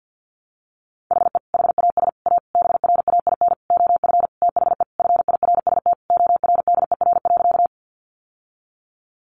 Morse code, heard as a keyed tone, about 49 wpm, 700 Hz. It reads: HE HAS A BRAIN OF THE FIRST ORDER.
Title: Sentences from Top 1100 Words 50wpm